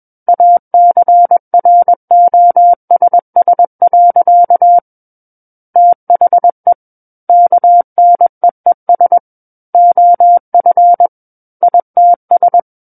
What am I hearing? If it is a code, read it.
Morse code, heard as a keyed tone, about 21 wpm, 700 Hz. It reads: ACROSS. THE KNEES OF ITS
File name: Book One / 12-Chapter 12 - 21 WPM